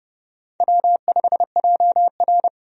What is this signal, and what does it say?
Morse code, heard as a keyed tone, about 30 wpm, 700 Hz. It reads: W5JR
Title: USA Call Signs - 4 Characters - 30wpm